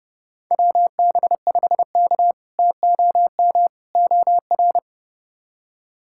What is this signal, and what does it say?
Morse code, heard as a keyed tone, about 30 wpm, 700 Hz. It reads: WB5K TOM OR
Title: K1USN SST QSO Practice - 30wpm